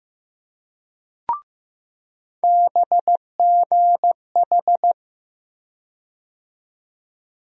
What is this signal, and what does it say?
Morse code, heard as a keyed tone, about 15 wpm, 700 Hz. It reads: BGH
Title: USA Call Sign Postfixes - 3 Letters 15wpm